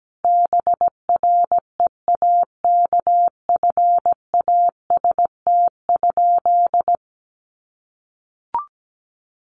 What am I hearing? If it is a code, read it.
Morse code, heard as a keyed tone, about 17 wpm, 700 Hz. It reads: BREAKFAST?
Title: Sentences from Top 1500 Words 17wpm